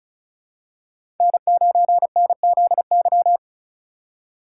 Morse code, heard as a keyed tone, about 36 wpm, 700 Hz. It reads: N9NZY
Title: USA Call Signs - 5 Characters - 2x Character Spacing - 35wpm